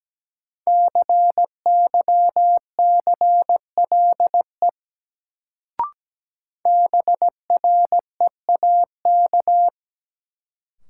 Morse code, heard as a keyed tone, about 17 wpm, 700 Hz. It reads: CYCLE BREAK
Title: Sentences from Top 900 Words - New Word Drill - 17wpm